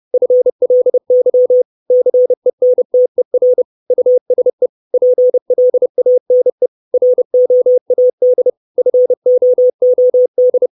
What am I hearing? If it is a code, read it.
Morse code, heard as a keyed tone, about 30 wpm, 500 Hz. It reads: FLY CENTER USE PLANE ROAD FOOD
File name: 632 Words 30WPM